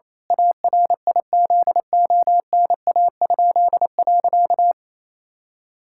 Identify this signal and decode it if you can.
Morse code, heard as a keyed tone, about 28 wpm, 700 Hz. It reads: ARIZONA?.
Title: Pangram Challenge v2 - 28wpm